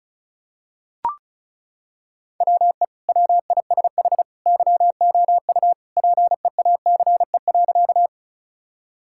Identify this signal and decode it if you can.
Morse code, heard as a keyed tone, about 36 wpm, 700 Hz. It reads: WE WISH YOU PEACE.
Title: Sentences from Top 1600 Words 35wpm